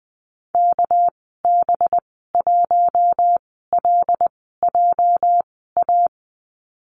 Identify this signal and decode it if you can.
Morse code, heard as a keyed tone, about 20 wpm, spaced as for 14 wpm, 700 Hz. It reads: KB1LJA
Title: 207 Lesson 33g - Letter j - Call Signs 20wpm